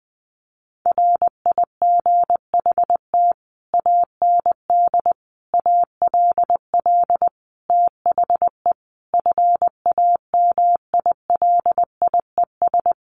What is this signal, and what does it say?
Morse code, heard as a keyed tone, about 20 wpm, 700 Hz. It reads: RIGHT AND ALL THE FAMILIES
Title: A Sound Of Thunder 20wpm